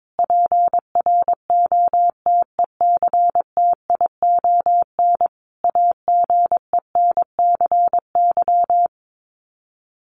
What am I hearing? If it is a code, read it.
Morse code, heard as a keyed tone, about 22 wpm, 700 Hz. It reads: PROTECTION AGENCY
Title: Sets of 5 Words Encore2 22wpm